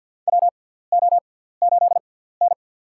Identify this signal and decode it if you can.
Morse code, heard as a keyed tone, about 48 wpm, spaced as for 17 wpm, 700 Hz. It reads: WO8N
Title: USA Call Signs - 6x Character Spacing - 50wpm